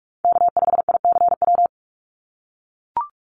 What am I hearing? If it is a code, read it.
Morse code, heard as a keyed tone, about 44 wpm, 700 Hz. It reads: K5ICW
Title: USA Call Signs 45wpm